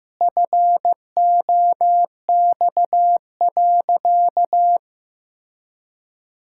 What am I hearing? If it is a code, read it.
Morse code, heard as a keyed tone, about 16 wpm, 700 Hz. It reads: FOX.
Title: Pangram Challenge v2 - 15wpm